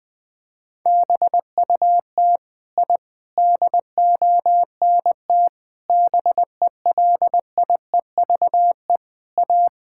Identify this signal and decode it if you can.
Morse code, heard as a keyed tone, about 20 wpm, 700 Hz. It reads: BUT I DONT BELIEVE A
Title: Sentences from Top 1000 Words - Review Part 6 - 20wpm